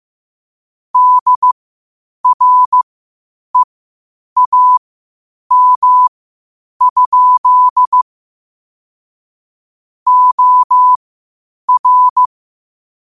Morse code, heard as a keyed tone, about 15 wpm, spaced as for 8 wpm, 1000 Hz. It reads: DREAM? OR
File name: qotdfiles / 8wpm